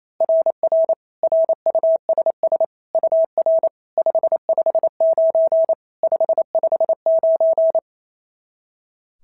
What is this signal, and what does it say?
Morse code, heard as a keyed tone, about 28 wpm, 690 Hz. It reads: RR RUSS UR 559 559
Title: SOTA QSO Practice 28wpm